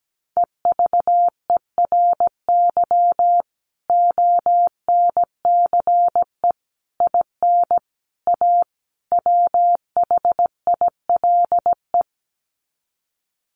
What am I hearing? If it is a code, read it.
Morse code, heard as a keyed tone, about 17 wpm, 700 Hz. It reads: EVERY ONCE IN A WHILE
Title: Sets of 5 Words 17wpm